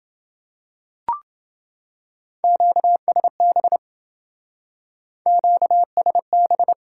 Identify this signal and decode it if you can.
Morse code, heard as a keyed tone, about 27 wpm, 700 Hz. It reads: QSB QSB